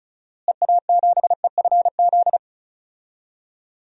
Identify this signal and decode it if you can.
Morse code, heard as a keyed tone, about 36 wpm, 700 Hz. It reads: EA7EFZ
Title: DXCC-100 Call Signs - 6 Characters - Repeated 2x - 35wpm